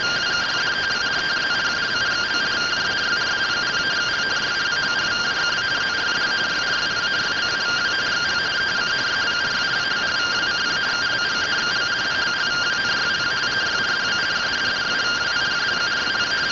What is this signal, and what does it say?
Two-tone teleprinter signal, about 170 Hz shift.